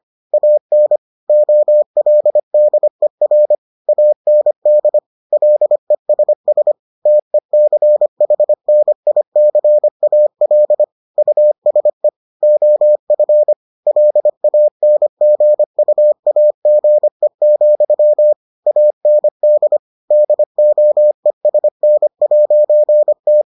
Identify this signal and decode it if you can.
Morse code, heard as a keyed tone, about 25 wpm, 600 Hz. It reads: AN OLDER AND LESS TECHNICAL USE OF LANGUAGE, AND DOESN'T